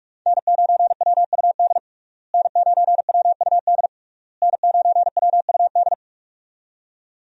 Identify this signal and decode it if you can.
Morse code, heard as a keyed tone, about 42 wpm, 700 Hz. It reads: N9WUD N9WUD N9WUD
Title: USA Call Signs - 5 Characters - Repeated 3x - 45wpm